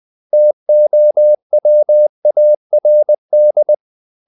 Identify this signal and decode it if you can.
Morse code, heard as a keyed tone, about 20 wpm, 600 Hz. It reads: TOWARD
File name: CW / Practice / 500MostCommonEnglishWords20